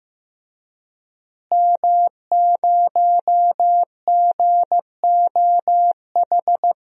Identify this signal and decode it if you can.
Morse code, heard as a keyed tone, about 15 wpm, 700 Hz. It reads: M0GOH
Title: DXCC-100 Call Signs - 5 Characters - Repeated 2x - 15wpm